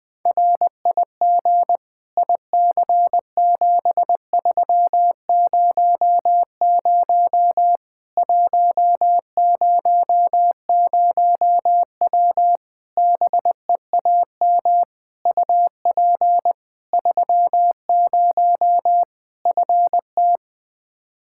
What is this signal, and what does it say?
Morse code, heard as a keyed tone, about 20 wpm, 700 Hz. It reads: RIG IC7300 100W BEAM UP 30 FT
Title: Practice QSO - Exchange 2 - Info - 20wpm